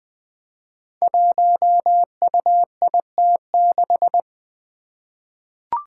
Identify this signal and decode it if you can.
Morse code, heard as a keyed tone, about 20 wpm, 700 Hz. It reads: 1UIT6
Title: Five Challenging Letter-Number Combinations 20wpm